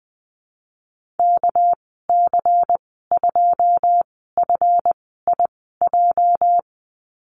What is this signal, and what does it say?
Morse code, heard as a keyed tone, about 20 wpm, spaced as for 15 wpm, 700 Hz. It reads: KC2FIJ
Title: 219 Lesson 35g - Number 0 - Call Signs 20wpm